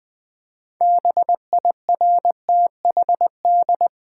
Morse code, heard as a keyed tone, about 20 wpm, 700 Hz. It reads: BIRTHD
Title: Build A Word - 3 Letters At A Time - 8-Letter Words - 20wpm